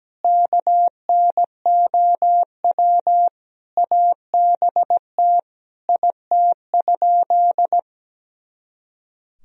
Morse code, heard as a keyed tone, about 17 wpm, 700 Hz. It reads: KNOW ABT IT?
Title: Sentences from Top 100 Words and Abbreviations 17wpm